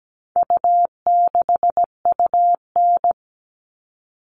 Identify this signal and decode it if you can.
Morse code, heard as a keyed tone, about 17 wpm, 700 Hz. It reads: U6UN
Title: All Cut Numbers - 4-Digit 17wpm